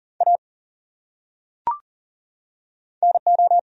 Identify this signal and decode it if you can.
Morse code, heard as a keyed tone, about 40 wpm, 700 Hz. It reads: A NO
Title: Top 100 Words 40wpm